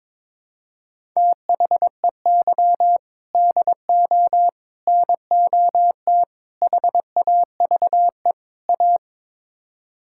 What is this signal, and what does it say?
Morse code, heard as a keyed tone, about 22 wpm, 700 Hz. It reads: THEY DO NOT HAVE A
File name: practice / 22wpm / Build A Phrase - 5-Word Phrases - 22wpm